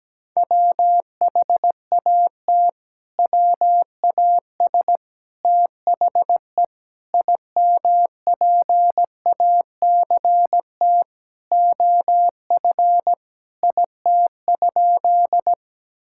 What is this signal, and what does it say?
Morse code, heard as a keyed tone, about 17 wpm, 700 Hz. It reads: WHAT WAS THE IMPACT OF IT?
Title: Sentences from Top 1000 Words - Review Part 6 - 17wpm